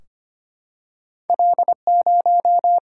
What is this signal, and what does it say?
Morse code, heard as a keyed tone, about 25 wpm, 700 Hz. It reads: L 0
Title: Single-Letter Single-Number 25wpm